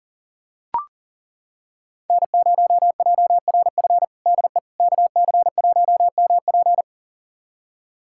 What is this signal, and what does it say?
Morse code, heard as a keyed tone, about 40 wpm, 700 Hz. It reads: N0JRF DE KC1MP
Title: USA Callsign DE Callsign - 5 Characters - 40wpm